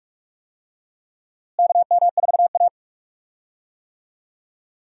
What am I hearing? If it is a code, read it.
Morse code, heard as a keyed tone, about 47 wpm, 700 Hz. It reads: KM4A
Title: USA Call Signs - 4 Characters - Repeated 2x - 45wpm